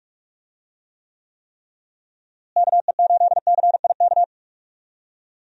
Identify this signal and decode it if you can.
Morse code, heard as a keyed tone, about 45 wpm, 700 Hz. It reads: KE8CIK
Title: USA Call Signs - 6 Characters - Repeated 3x - 45wpm